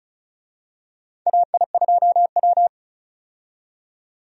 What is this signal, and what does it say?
Morse code, heard as a keyed tone, about 36 wpm, 700 Hz. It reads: AI2W
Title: USA Call Signs - 4 Characters - 35wpm